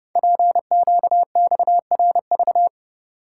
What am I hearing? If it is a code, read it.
Morse code, heard as a keyed tone, about 30 wpm, 700 Hz. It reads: PQXRV
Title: Code Groups - 5 Letters - 30wpm